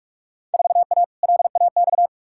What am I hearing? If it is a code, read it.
Morse code, heard as a keyed tone, about 43 wpm, 700 Hz. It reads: 4A LAX
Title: Field Day Exchanges - 45wpm